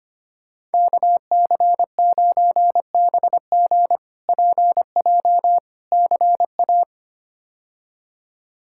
Morse code, heard as a keyed tone, about 25 wpm, 700 Hz. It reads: KC9BG PJ CA
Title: K1USN SST QSO Practice - 25wpm